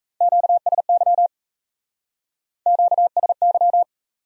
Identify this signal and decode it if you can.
Morse code, heard as a keyed tone, about 38 wpm, 700 Hz. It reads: QSY QSY